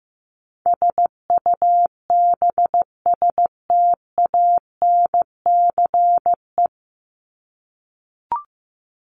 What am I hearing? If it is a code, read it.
Morse code, heard as a keyed tone, about 15 wpm, 700 Hz. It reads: SUBSTANCE
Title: Top 1251-1500 Words 15wpm